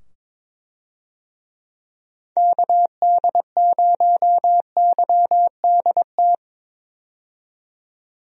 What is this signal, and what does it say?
Morse code, heard as a keyed tone, about 22 wpm, 700 Hz. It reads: KD0YDT
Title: USA Call Signs - 6 Characters - 22wpm